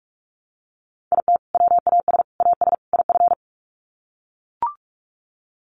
Morse code, heard as a keyed tone, about 44 wpm, 700 Hz. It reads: IT WAS AS IF
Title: Sets of 4 Words 45wpm